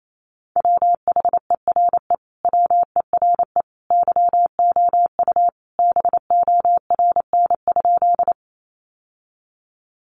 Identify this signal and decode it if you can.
Morse code, heard as a keyed tone, about 28 wpm, 700 Hz. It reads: WHERE WERE YOU BORN?
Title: Sentences from Top 1000 Words 28wpm